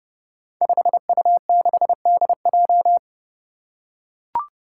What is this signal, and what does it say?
Morse code, heard as a keyed tone, about 30 wpm, 700 Hz. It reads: 5U6DJ
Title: Five Challenging Letter-Number Combinations 30wpm